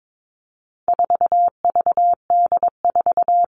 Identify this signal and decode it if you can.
Morse code, heard as a keyed tone, about 22 wpm, 700 Hz. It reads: 4VD4
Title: All Cut Numbers - 4-Digit 22wpm